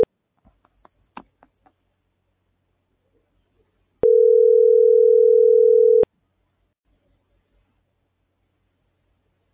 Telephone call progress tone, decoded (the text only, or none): ringback tone